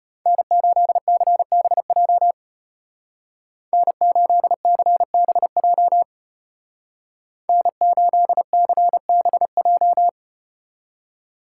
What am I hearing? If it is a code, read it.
Morse code, heard as a keyed tone, about 34 wpm, 700 Hz. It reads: N8CBJ N8CBJ N8CBJ